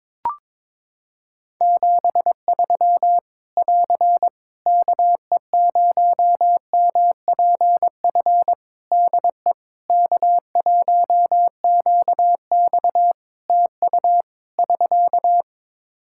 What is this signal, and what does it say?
Morse code, heard as a keyed tone, about 22 wpm, 700 Hz. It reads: 73 <AR> KE0MPF DE K1QX TU <SK>
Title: Practice QSO - Exchange 4 - Ending - 22wpm